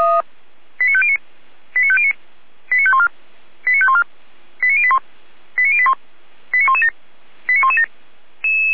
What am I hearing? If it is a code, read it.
Five-tone selective call (ZVEI), heard as quick runs of five tones, tones of about 70 ms. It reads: 89509 89509 89514 89514 89081 89081 89117 89117
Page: SDR/ZVEI